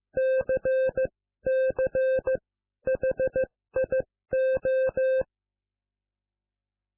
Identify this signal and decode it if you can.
Morse code, heard as a keyed tone, about 15 wpm, spaced as for 12 wpm, 520 Hz. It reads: CCHIO